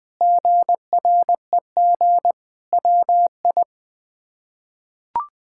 Morse code, heard as a keyed tone, about 20 wpm, 700 Hz. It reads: GREG WI
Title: K1USN SST QSO Practice - 20wpm